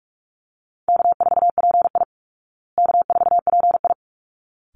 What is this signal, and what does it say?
Morse code, heard as a keyed tone, about 44 wpm, 700 Hz. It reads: K4PI K4PI
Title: Easy Call Signs Repeated 2x - 45wpm